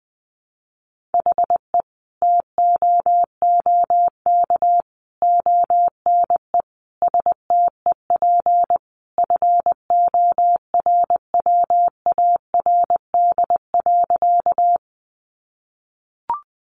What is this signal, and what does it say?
Morse code, heard as a keyed tone, about 20 wpm, 700 Hz. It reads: HE TOOK ONE STEP FORWARD.
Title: Sentences from Top 1300 Words 20wpm